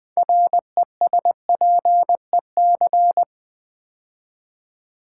Morse code, heard as a keyed tone, about 20 wpm, 700 Hz. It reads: RESPEC
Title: Build A Word - 6 Letters At A Time - 12-Letter Words - 20wpm